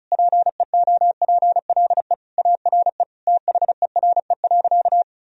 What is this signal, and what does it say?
Morse code, heard as a keyed tone, about 36 wpm, 700 Hz. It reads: PEOPLE ARE THERE.
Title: Sentences from Top 1000 Words - Review Part 1 - 35wpm